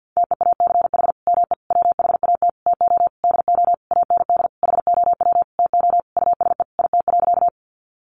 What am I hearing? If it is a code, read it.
Morse code, heard as a keyed tone, about 50 wpm, 700 Hz. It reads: TEACH ME WHAT TO DO AND HOW TO USE IT.
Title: Sentences from Top 500 Words 50wpm